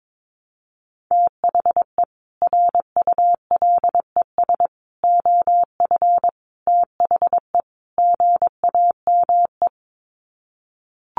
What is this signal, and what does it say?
Morse code, heard as a keyed tone, about 22 wpm, 700 Hz. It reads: THE RULES OF THE GAME
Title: Sets of 5 Words Encore2 22wpm